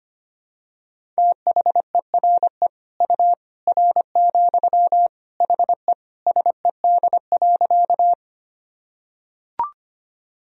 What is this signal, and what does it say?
Morse code, heard as a keyed tone, about 25 wpm, 700 Hz. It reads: THERE U R, HE SED.
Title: Sentences from Top 100 Words and Abbreviations 25wpm